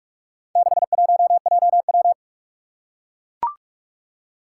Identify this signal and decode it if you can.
Morse code, heard as a keyed tone, about 44 wpm, 700 Hz. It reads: 61JW
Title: Four Challenging Letter-Number Combinations 45wpm